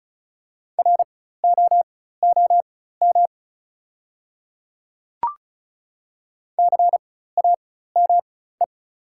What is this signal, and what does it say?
Morse code, heard as a keyed tone, about 36 wpm, spaced as for 16 wpm, 690 Hz. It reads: ROOM CAME
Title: 4x Character Spacing - Top 300 Words - 35wpm